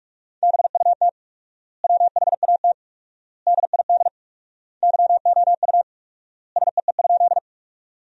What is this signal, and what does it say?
Morse code, heard as a keyed tone, about 46 wpm, 700 Hz. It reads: BUT WHAT DID YOU SEE?
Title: 4x Word-Spacing - Sentences from Top 100 Words - 45wpm